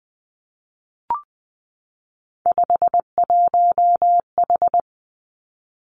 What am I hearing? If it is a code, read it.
Morse code, heard as a keyed tone, about 20 wpm, 700 Hz. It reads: 51H
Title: Three Challenging Letter-Number Combinations 20wpm